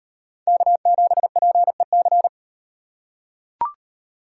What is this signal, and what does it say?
Morse code, heard as a keyed tone, about 38 wpm, 700 Hz. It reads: K7PEC